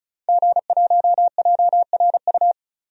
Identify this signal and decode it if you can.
Morse code, heard as a keyed tone, about 34 wpm, 700 Hz. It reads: G1JRU
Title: DXCC-100 Call Signs - 5 Characters - 35wpm